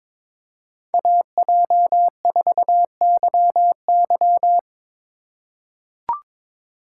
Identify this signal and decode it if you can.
Morse code, heard as a keyed tone, about 22 wpm, 700 Hz. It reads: AJ4YY
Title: USA Call Signs - 5 Characters - 3x Character Spacing - 22wpm